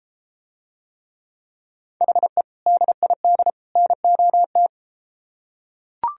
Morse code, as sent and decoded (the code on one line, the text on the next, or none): .... / -.. .. -.. / -. --- -
HE DID NOT